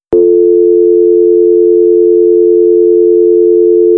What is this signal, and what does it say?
Dial tone (North America), 350 + 440 Hz.
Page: Dial-Tone-American